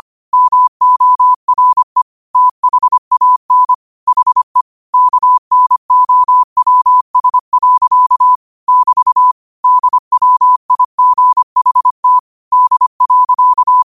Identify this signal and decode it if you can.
Morse code, heard as a keyed tone, about 25 wpm, 1000 Hz. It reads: MORE THAN HE KNOWS. <BT> DWIGHT D.
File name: qotdfiles / 25wpm